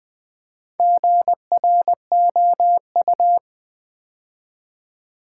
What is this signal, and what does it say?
Morse code, heard as a keyed tone, about 20 wpm, 700 Hz. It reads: GROU
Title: Build A Word - 2 Letters At A Time - 6-Letter Words - 20wpm